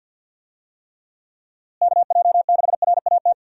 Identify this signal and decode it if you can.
Morse code, heard as a keyed tone, about 52 wpm, 700 Hz. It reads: KJ6RAT
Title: USA Call Signs - 6 Characters - Repeated 3x - 50wpm